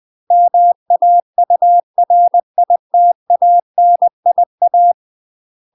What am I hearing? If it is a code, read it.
Morse code, heard as a keyed tone, about 20 wpm, 700 Hz. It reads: MAURITANIA
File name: UK MASTER QSOs TRAINING / 20WPM / UK MASTER QSOs 0035